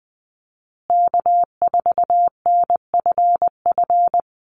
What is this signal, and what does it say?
Morse code, heard as a keyed tone, about 20 wpm, 700 Hz. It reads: K4NFF
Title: USA Call Signs Repeated 2x 20wpm